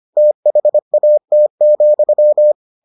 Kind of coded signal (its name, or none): Morse code